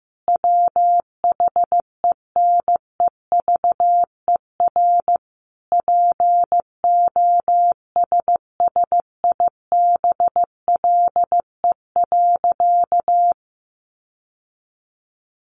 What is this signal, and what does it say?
Morse code, heard as a keyed tone, about 15 wpm, 700 Hz. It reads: WHENEVER POSSIBLE.
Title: Sentences from Top 1200 Words 15wpm